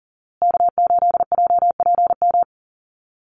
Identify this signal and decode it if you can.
Morse code, heard as a keyed tone, about 40 wpm, 700 Hz. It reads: K8JPM
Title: Call Signs Repeated 2x - 40wpm